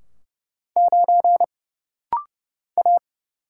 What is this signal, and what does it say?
Morse code, heard as a keyed tone, about 30 wpm, 700 Hz. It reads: 9 A
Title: Single Letter-Number - Rapid-Fire 30wpm